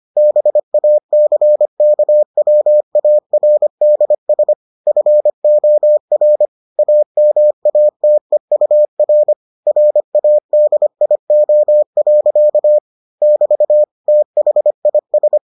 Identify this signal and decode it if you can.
Morse code, heard as a keyed tone, about 25 wpm, 600 Hz. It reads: BACKWARDS FOR AMATEUR RADIO. <BT> THIS